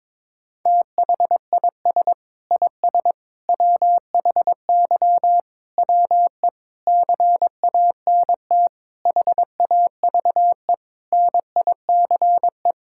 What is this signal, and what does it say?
Morse code, heard as a keyed tone, about 22 wpm, 700 Hz. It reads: THIS IS WHY WE CANT HAVE NICE
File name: practice / 22wpm / Sentences with 8-Word Phrases - 22wpm